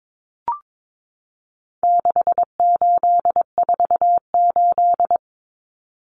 Morse code, heard as a keyed tone, about 22 wpm, 700 Hz. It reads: 6848
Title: Numbers - 4-Digit 22wpm